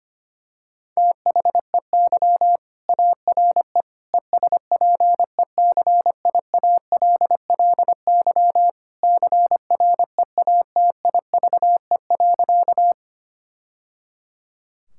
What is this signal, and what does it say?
Morse code, heard as a keyed tone, about 25 wpm, 700 Hz. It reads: THEY ARE ESPECIALLY CREATIVE.
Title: Sentences from Top 1000 Words - Review Part 5 - 25wpm